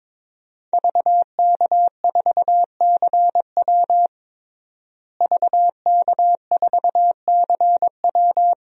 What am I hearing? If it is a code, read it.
Morse code, heard as a keyed tone, about 22 wpm, 700 Hz. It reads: VK4CW VK4CW
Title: DXCC-100 Call Signs - 5 Characters - Repeated 2x - 22wpm